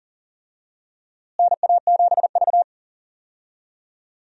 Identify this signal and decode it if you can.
Morse code, heard as a keyed tone, about 39 wpm, 700 Hz. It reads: NA7V
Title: USA Call Signs - 4 Characters - Repeated 2x - 40wpm